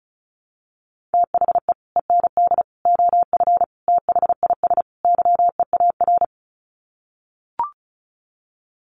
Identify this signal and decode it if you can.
Morse code, heard as a keyed tone, about 34 wpm, 700 Hz. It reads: THE END OF THIS YEAR
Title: Sets of 5 Words Encore 35wpm